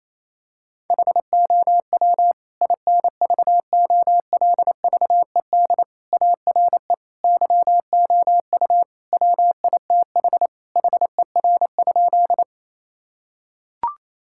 Morse code, heard as a keyed tone, about 28 wpm, 700 Hz. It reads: HOW INVOLVED ARE YOU WITH HER?
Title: Sentences from Top 1000 Words - Review Part 5 - 28wpm